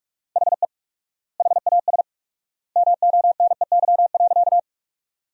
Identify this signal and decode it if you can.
Morse code, heard as a keyed tone, about 42 wpm, 700 Hz. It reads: HE HAS MONEY.
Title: 4x Word-Spacing - Sentences from Top 400 Words - 45wpm